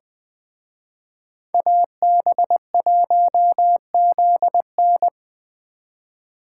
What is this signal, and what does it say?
Morse code, heard as a keyed tone, about 20 wpm, 700 Hz. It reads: AB1ZN